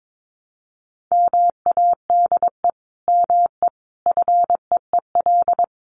Morse code, heard as a keyed tone, about 22 wpm, 700 Hz. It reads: MADE ME FEEL
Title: Sentences from Top 400 Words 22wpm